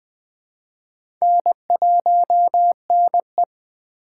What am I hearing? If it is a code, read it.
Morse code, heard as a keyed tone, about 20 wpm, 700 Hz. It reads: N1NE
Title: USA Call Signs - 4x Character Spacing - 20wpm